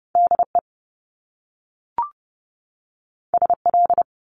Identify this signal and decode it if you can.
Morse code, heard as a keyed tone, about 30 wpm, 700 Hz. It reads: DE SL
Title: Top Two Letter Combinations in Words 30wpm